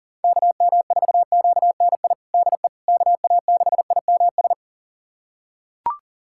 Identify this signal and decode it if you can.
Morse code, heard as a keyed tone, about 37 wpm, 700 Hz. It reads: KM4QNI DE KA6IMS
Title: USA Callsign DE Callsign - 6 Characters - 40wpm